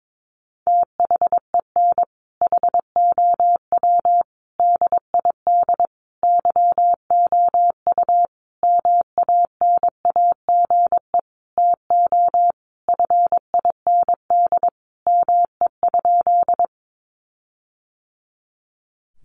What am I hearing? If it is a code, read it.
Morse code, heard as a keyed tone, about 22 wpm, 700 Hz. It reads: THEN HOW DID YOU MANAGE TO FIND ME?
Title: Sentences from Top 1400 Words 22wpm